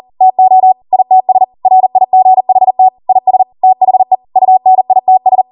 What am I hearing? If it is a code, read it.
Morse code, heard as a keyed tone, about 40 wpm, 750 Hz. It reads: TO ITS RIGHT IS THE UNITS